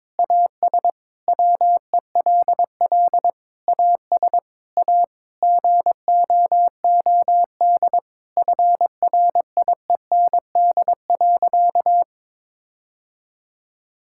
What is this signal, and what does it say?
Morse code, heard as a keyed tone, about 22 wpm, 700 Hz. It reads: AS WELL AS A GOOD FRIEND.
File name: practice / 22wpm / Sentences with 4-Word Phrases - 22wpm